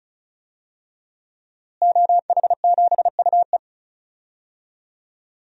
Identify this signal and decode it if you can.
Morse code, heard as a keyed tone, about 34 wpm, 700 Hz. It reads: OH7UE
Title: DXCC-100 Call Signs - 5 Characters - Repeated 3x - 35wpm